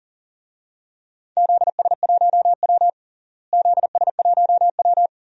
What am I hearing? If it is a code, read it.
Morse code, heard as a keyed tone, about 40 wpm, 700 Hz. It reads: ZS1W ZS1W
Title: DXCC-100 Call Signs - 4 Characters - Repeated 2x - 40wpm